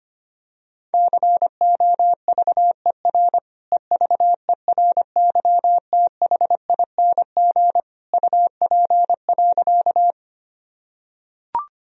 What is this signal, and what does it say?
Morse code, heard as a keyed tone, about 25 wpm, 700 Hz. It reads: COVER EVERYTHING UP.
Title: Sentences from Top 1000 Words - Review Part 1 - 25wpm